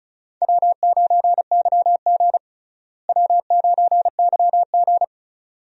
Morse code, heard as a keyed tone, about 34 wpm, 700 Hz. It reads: W9YG W9YG
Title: USA Call Signs - 4 Characters - Repeated 2x - 35wpm